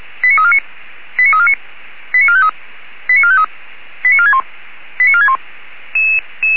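Five-tone selective call (ZVEI), heard as quick runs of five tones, tones of about 80 ms. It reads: 89349 89349 89453 89453 89461 89461
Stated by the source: ZVEI five-tone selective call received by SDR.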